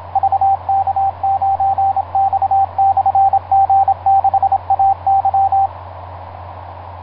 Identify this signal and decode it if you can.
Morse code, heard as a keyed tone, about 27 wpm, 780 Hz, with radio noise. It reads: VK9X/G6AY
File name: vk9x-g6ay ha7lw etc 10m cw 23feb2010 0836